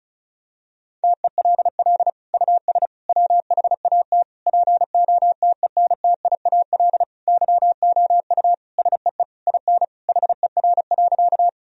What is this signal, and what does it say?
Morse code, heard as a keyed tone, about 36 wpm, 700 Hz. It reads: TELL US WHAT POTENTIAL YOU SEE IN HER.
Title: Sentences from Top 1000 Words - Review Part 4 - 35wpm